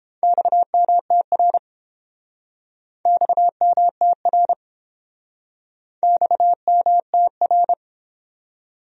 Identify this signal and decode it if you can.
Morse code, heard as a keyed tone, about 32 wpm, 700 Hz. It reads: XMTR XMTR XMTR